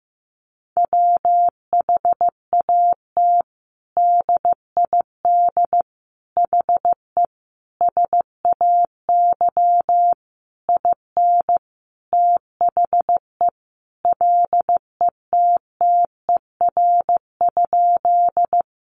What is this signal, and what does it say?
Morse code, heard as a keyed tone, about 15 wpm, 700 Hz. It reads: WHAT DID HE SAY IN THE LETTER?
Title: Sentences from Top 300 Words 15wpm